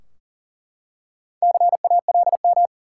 Morse code, heard as a keyed tone, about 40 wpm, 700 Hz. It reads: CALM
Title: Four Letter Words - 40wpm